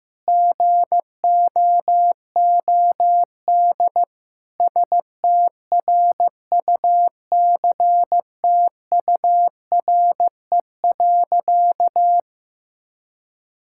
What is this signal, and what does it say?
Morse code, heard as a keyed tone, about 15 wpm, 700 Hz. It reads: GOOD STRUCTURE.
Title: Sentences from Top 1000 Words - Review Part 6 - 15wpm